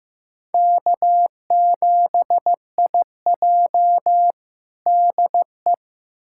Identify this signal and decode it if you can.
Morse code, heard as a keyed tone, about 15 wpm, 700 Hz. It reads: K7IJ DE
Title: USA Callsign DE Callsign - 4 Characters - 15wpm